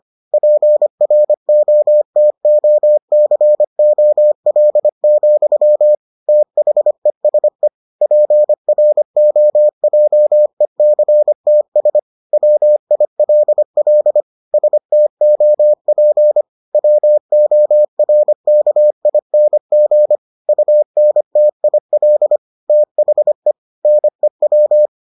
Morse code, heard as a keyed tone, about 25 wpm, 600 Hz. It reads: PROTOCOL, THESE PROJECTS WILL STOP WORKING UNTIL THE NEW